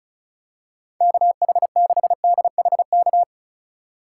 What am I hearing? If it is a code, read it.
Morse code, heard as a keyed tone, about 36 wpm, 700 Hz. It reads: KH6DHK
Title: K1USN SST QSO Practice - 35wpm